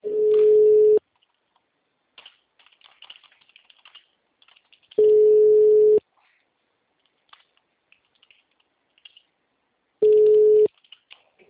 Ringback tone (Europe, 425 Hz).
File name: out- -2025- - - .49623